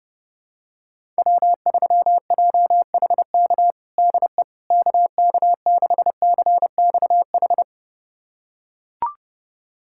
Morse code, heard as a keyed tone, about 30 wpm, 700 Hz. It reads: W3JHK DE KK6CXH
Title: USA Callsign DE Callsign 30wpm